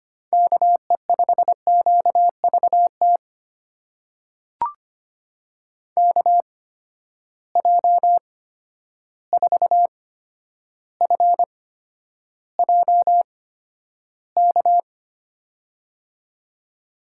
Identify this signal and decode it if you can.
Morse code, heard as a keyed tone, about 25 wpm, spaced as for 11 wpm, 700 Hz. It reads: KE5QVT KJ4FJK
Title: USA Call Signs - 6 Characters - 8x Character Spacing - 25wpm